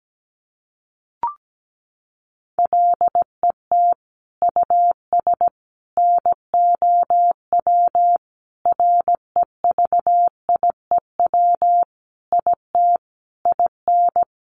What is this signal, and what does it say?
Morse code, heard as a keyed tone, about 17 wpm, 700 Hz. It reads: LET US NOW REVIEW IT IN
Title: Sentences from Top 1500 Words 17wpm